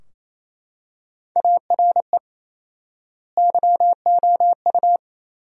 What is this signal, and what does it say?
Morse code, heard as a keyed tone, about 28 wpm, 700 Hz. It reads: ARE YOU
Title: 4x Word-Spacing - Sentences from Top 400 Words - 28wpm